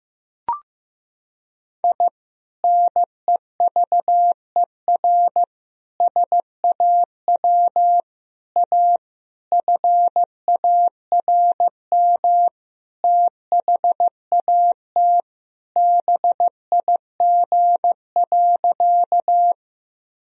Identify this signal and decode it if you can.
Morse code, heard as a keyed tone, about 15 wpm, 700 Hz. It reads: I NEVER SAW A FARM THAT BIG.
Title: Sentences from Top 1000 Words - Review Part 2 - 15wpm